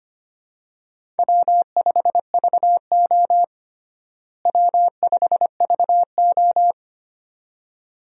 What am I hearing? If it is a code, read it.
Morse code, heard as a keyed tone, about 25 wpm, 700 Hz. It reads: W5VO W5VO
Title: USA Call Signs - 4 Characters - Repeated 2x - 25wpm